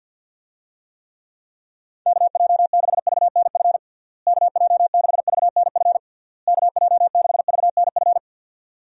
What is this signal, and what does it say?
Morse code, heard as a keyed tone, about 52 wpm, 700 Hz. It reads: KJ6VNF KJ6VNF KJ6VNF
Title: USA Call Signs - 6 Characters - Repeated 3x - 50wpm